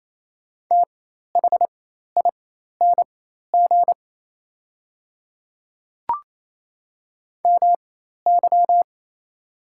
Morse code, heard as a keyed tone, about 28 wpm, spaced as for 13 wpm, 700 Hz. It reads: THING MY
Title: 4x Character Spacing - Top 100 Words - 28wpm